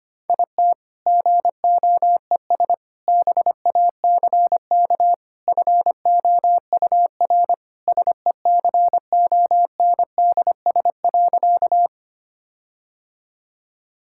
Morse code, heard as a keyed tone, about 25 wpm, 700 Hz. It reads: IT GOES BACK FOUR SECONDS.
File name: practice / 25wpm / All-Age Appropriate Jokes - 25wpm